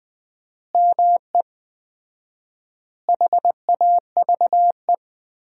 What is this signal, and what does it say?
Morse code, heard as a keyed tone, about 20 wpm, 700 Hz. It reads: ME HAVE
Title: 4x Word-Spacing - Sentences from Top 300 Words - 20wpm